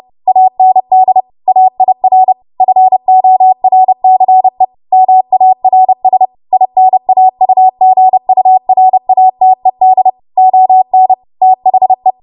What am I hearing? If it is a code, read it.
Morse code, heard as a keyed tone, about 30 wpm, 750 Hz. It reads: AND AIR FORCE MARS INAUGURATED ON THE
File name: Archive / 30 WPM / 30WPM